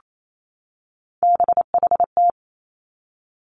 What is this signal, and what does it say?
Morse code, heard as a keyed tone, about 28 wpm, 700 Hz. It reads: BHT
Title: Three Challenging Letter-Number Combinations 28wpm